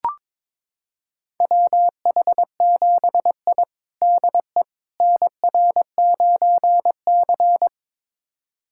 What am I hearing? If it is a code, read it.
Morse code, heard as a keyed tone, about 22 wpm, 700 Hz. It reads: WH7I DE NR9C
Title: USA Callsign DE Callsign - 4 Characters - 22wpm